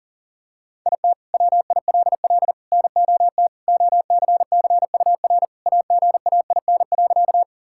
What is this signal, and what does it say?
Morse code, heard as a keyed tone, about 40 wpm, 700 Hz. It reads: IT WILL NOT OCCUR AGAIN.
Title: Sentences from Top 1000 Words - Review Part 6 - 40wpm